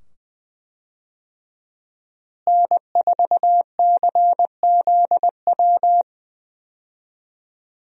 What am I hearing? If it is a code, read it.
Morse code, heard as a keyed tone, about 20 wpm, 700 Hz. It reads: N4CZW
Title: USA Call Signs - 5 Characters - Repeated 2x - 20wpm